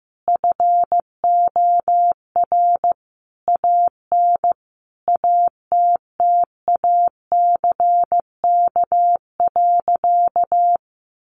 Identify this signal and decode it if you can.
Morse code, heard as a keyed tone, about 15 wpm, 700 Hz. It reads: FOR AN ATTACK.
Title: Sentences from Top 1500 Words 15wpm